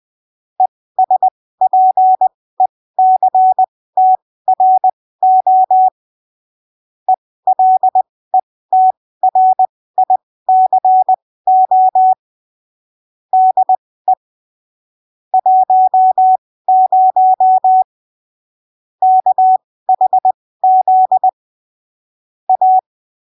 Morse code, heard as a keyed tone, about 20 wpm, spaced as for 15 wpm, 750 Hz. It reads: ESPECTRO ELETRICO DE 10 KHZ A
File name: arquivos / texto / texto 12 20ppm